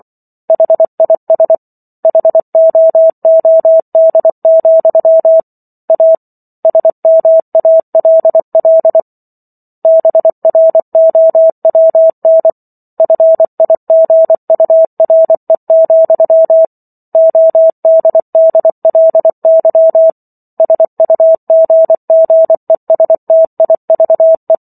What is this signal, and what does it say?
Morse code, heard as a keyed tone, about 24 wpm, 650 Hz. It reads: HIS HOOD, A SMALL BROWN FIGURE, ODDLY SUGGESTIVE
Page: War of the Worlds - 15-Chapter 15 - 24 WPM